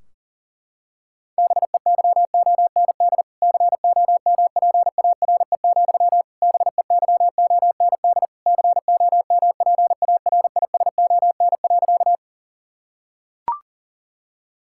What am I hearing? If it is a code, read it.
Morse code, heard as a keyed tone, about 37 wpm, 700 Hz. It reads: BEYOND COMPARE, BEYOND COMPARISON.
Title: Sentences from Top 1600 Words 40wpm